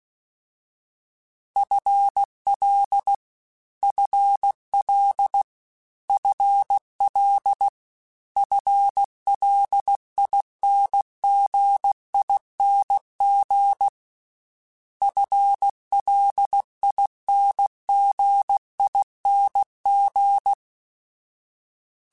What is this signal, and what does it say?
Morse code, heard as a keyed tone, about 16 wpm, 800 Hz. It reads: FL FL FL FLINGING FLINGING